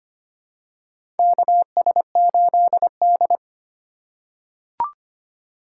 Morse code, heard as a keyed tone, about 25 wpm, 700 Hz. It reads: KS8D
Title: USA Call Signs - 4 Characters - 2x Character Spacing - 25wpm